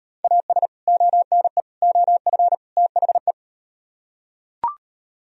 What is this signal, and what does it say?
Morse code, heard as a keyed tone, about 38 wpm, 700 Hz. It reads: AS ONE OF THE